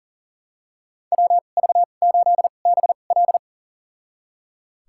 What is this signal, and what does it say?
Morse code, heard as a keyed tone, about 38 wpm, spaced as for 28 wpm, 700 Hz. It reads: WV8BL
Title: USA Call Signs - 5 Characters - 2x Character Spacing - 40wpm